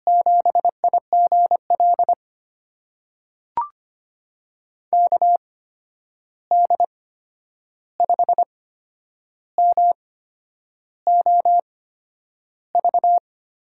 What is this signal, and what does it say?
Morse code, heard as a keyed tone, about 25 wpm, spaced as for 9 wpm, 700 Hz. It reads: 7IGL KD5MOV
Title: USA Call Signs - 6 Characters - 8x Character Spacing - 25wpm